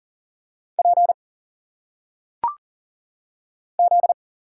Morse code, heard as a keyed tone, about 40 wpm, 700 Hz. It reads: P Z
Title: Single Letters - Phonetic Pronunciation 40wpm